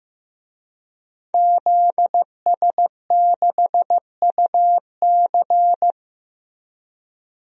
Morse code, heard as a keyed tone, about 15 wpm, 700 Hz. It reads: ZS6UC
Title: DXCC-100 Call Signs - 5 Characters - Repeated 2x - 15wpm